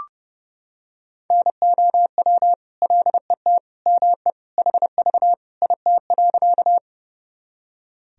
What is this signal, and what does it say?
Morse code, heard as a keyed tone, about 32 wpm, 700 Hz. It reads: NOW LET ME HV IT.
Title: Sentences from Top 300 Words and Abbreviations 30wpm